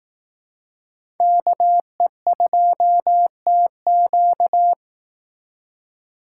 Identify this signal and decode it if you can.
Morse code, heard as a keyed tone, about 19 wpm, 700 Hz. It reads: KE2TQ